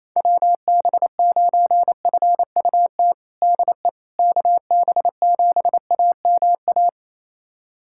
Morse code, heard as a keyed tone, about 28 wpm, 700 Hz. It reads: WB9FUT DE KB7AMA
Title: USA Callsign DE Callsign - 6 Characters - 28wpm